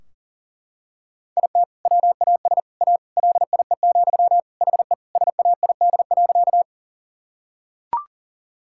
Morse code, heard as a keyed tone, about 40 wpm, 700 Hz. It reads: IT WAS A LIE, HE SAID.
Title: Sentences from Top 1600 Words 40wpm